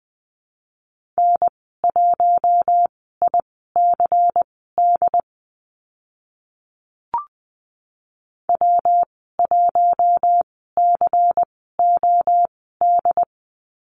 Morse code, heard as a keyed tone, about 20 wpm, spaced as for 14 wpm, 700 Hz. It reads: N1ICD W1COD
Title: 128 Lesson 21g - Number 6 - Call Signs 20wpm